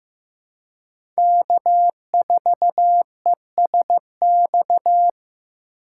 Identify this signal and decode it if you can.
Morse code, heard as a keyed tone, about 15 wpm, 700 Hz. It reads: K4ESX
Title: USA Call Signs - 5 Characters - 3x Character Spacing - 15wpm